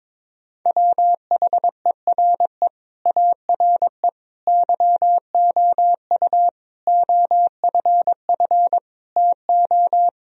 Morse code, heard as a keyed tone, about 22 wpm, 700 Hz. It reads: WHERE ARE YOU OFF TO
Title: Sentences from Top 1000 Words - Review Part 2 - 22wpm